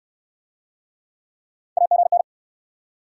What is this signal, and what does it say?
Morse code, heard as a keyed tone, about 67 wpm, 700 Hz. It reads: AGN